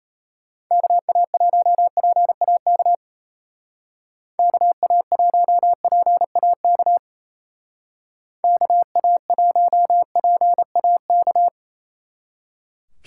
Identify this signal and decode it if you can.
Morse code, heard as a keyed tone, about 35 wpm, 700 Hz. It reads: KA1PAK KA1PAK KA1PAK
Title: Speed Racing - USA 2x3 Call Signs 28wpm